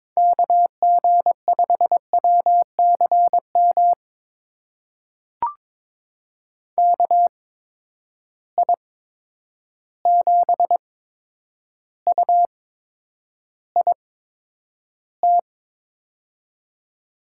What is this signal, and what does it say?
Morse code, heard as a keyed tone, about 21 wpm, spaced as for 9 wpm, 700 Hz. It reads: KG5WCM KI7UIT
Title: USA Call Signs - 6 Characters - 8x Character Spacing - 22wpm